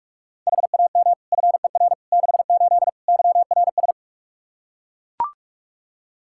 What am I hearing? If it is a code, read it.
Morse code, heard as a keyed tone, about 41 wpm, 700 Hz. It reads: HAM FER 68 YRS
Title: K1USN SST QSO Practice 45wpm